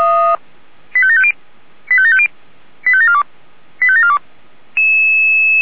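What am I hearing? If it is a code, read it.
Five-tone selective call (ZVEI), heard as quick runs of five tones, tones of about 70 ms. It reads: 85750 85750 85742 85742
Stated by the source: ZVEI five-tone selective call received on SDR.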